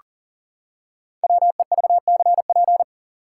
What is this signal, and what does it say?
Morse code, heard as a keyed tone, about 39 wpm, 700 Hz. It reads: WEVCP
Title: Code Groups - 5 Letter:Numbers - 40wpm